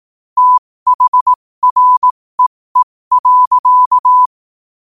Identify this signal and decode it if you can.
Morse code, heard as a keyed tone, about 18 wpm, spaced as for 15 wpm, 1000 Hz. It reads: THREE.
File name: qotdfiles / 15wpm